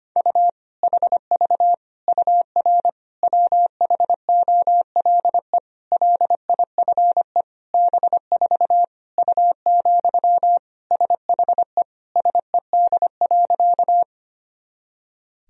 Morse code, heard as a keyed tone, about 25 wpm, 700 Hz. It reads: U HV UR WHOLE LIFE B4 U, SHE SED.
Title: Sentences from Top 400 Words and Abbreviations 25wpm